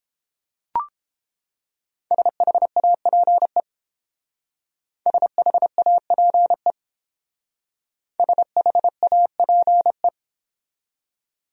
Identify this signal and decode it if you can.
Morse code, heard as a keyed tone, about 30 wpm, 700 Hz. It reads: SHAPE SHAPE SHAPE